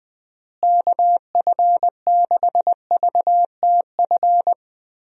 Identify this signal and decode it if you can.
Morse code, heard as a keyed tone, about 20 wpm, 700 Hz. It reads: KF6VTF
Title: USA Call Signs - 6 Characters - Repeated 3x - 20wpm